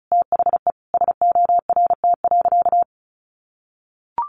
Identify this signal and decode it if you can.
Morse code, heard as a keyed tone, about 34 wpm, 700 Hz. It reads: THE SORT.
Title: Sentences from Top 1300 Words 35wpm